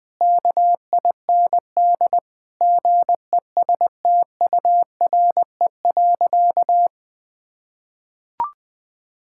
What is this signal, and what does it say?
Morse code, heard as a keyed tone, about 21 wpm, 700 Hz. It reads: KIND GESTURE.
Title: Difficult Sentences with Hints - 20wpm